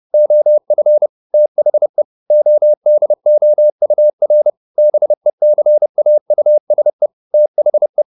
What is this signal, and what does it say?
Morse code, heard as a keyed tone, about 30 wpm, 600 Hz. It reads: OF THE ODOUR BECAUSE THE
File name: the old man and the sea / the old man and the sea 1 30wpm